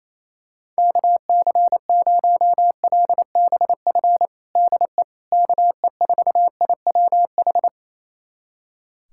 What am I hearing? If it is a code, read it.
Morse code, heard as a keyed tone, about 28 wpm, 700 Hz. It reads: KC0LBF DE KE4IWH
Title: USA Callsign DE Callsign - 6 Characters - 28wpm